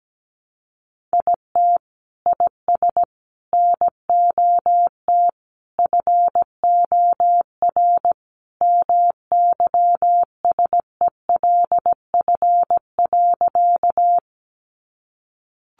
Morse code, heard as a keyed tone, about 17 wpm, 700 Hz. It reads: IT IS NOT FOR MYSELF.
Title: Sentences from Top 1300 Words 17wpm